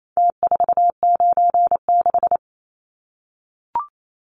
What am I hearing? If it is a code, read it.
Morse code, heard as a keyed tone, about 28 wpm, 700 Hz. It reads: T496
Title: Numbers - Contest Style - 4-Digit 28wpm